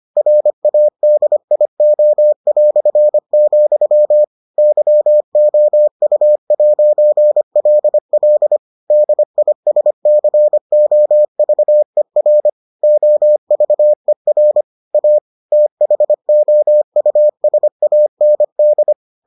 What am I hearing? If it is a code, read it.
Morse code, heard as a keyed tone, about 25 wpm, 600 Hz. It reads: RADIO", YOU'LL DISCOVER OVER A THOUSAND